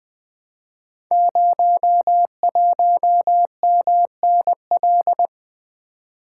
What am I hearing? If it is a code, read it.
Morse code, heard as a keyed tone, about 20 wpm, 700 Hz. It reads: 01MNL
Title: Code Groups - 5 Letter:Numbers - 20wpm